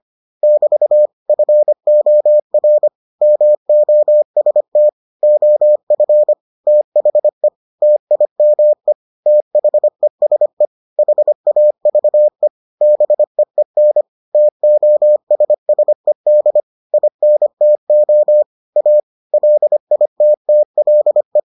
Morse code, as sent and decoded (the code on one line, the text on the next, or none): -...- / ..-. --- .-. / -- --- ... - / --- ..-. / - .... / - .. -- . / - .... ... / .... .- ...- . / -... . -. / - --- ... ... -.. / .. -. - --- / .- / .-.. .. - - .-.. .
<BT> FOR MOST OF THE TIME THESE HAVE BEEN TOSSED INTO A LITTLE